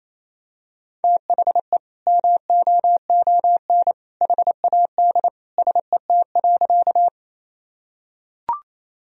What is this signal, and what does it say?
Morse code, heard as a keyed tone, about 28 wpm, 700 Hz. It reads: THE MOON HAD SET.
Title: Sentences from Top 1000 Words - Review Part 3 - 28wpm